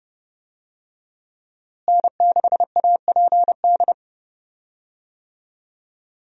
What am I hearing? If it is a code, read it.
Morse code, heard as a keyed tone, about 32 wpm, 700 Hz. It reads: N6APD
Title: USA Call Signs - 5 Characters - Repeated 2x - 30wpm